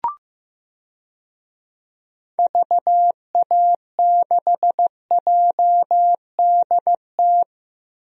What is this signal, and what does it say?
Morse code, heard as a keyed tone, about 15 wpm, 700 Hz. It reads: VA6JDT
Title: DXCC-100 Call Signs - 6 Characters - Repeated 3x - 15wpm